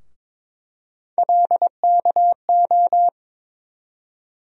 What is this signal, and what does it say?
Morse code, heard as a keyed tone, about 22 wpm, 700 Hz. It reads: LKO
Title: USA Call Sign Postfixes - 3 Letters 22wpm